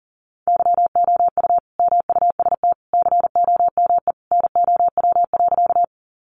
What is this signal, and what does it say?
Morse code, heard as a keyed tone, about 40 wpm, 700 Hz. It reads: YOU MUST COME NOW.
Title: Sentences from Top 200 Words 40wpm